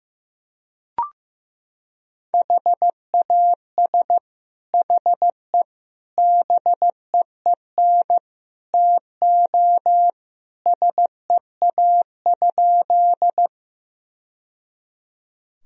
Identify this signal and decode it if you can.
Morse code, heard as a keyed tone, about 15 wpm, 700 Hz. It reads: HAS HE BEEN TO SEA?
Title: Sentences from Top 1000 Words - Review Part 1 - 15wpm